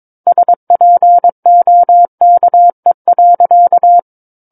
Morse code, heard as a keyed tone, about 22 wpm, 700 Hz. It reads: SPOKE.
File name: 06-Chapter-06 22 WPM